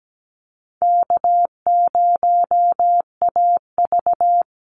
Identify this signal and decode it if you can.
Morse code, heard as a keyed tone, about 17 wpm, 700 Hz. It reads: K0AV
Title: Call Signs Repeated 2x - 17wpm